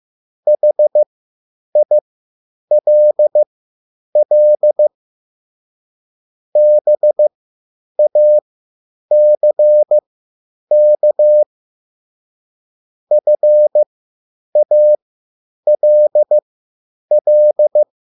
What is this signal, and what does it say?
Morse code, heard as a keyed tone, about 15 wpm, spaced as for 9 wpm, 600 Hz. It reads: HILL BACK FALL
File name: QSL / CW lesson